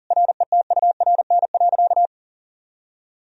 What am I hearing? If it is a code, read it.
Morse code, heard as a keyed tone, about 37 wpm, 700 Hz. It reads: RETURN.
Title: Sentences from Top 1000 Words - Review Part 1 - 40wpm